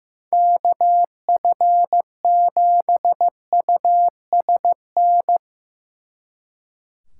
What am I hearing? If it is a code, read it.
Morse code, heard as a keyed tone, about 17 wpm, 700 Hz. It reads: KF7USN
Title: Speed Racing - USA 2x3 Call Signs 15wpm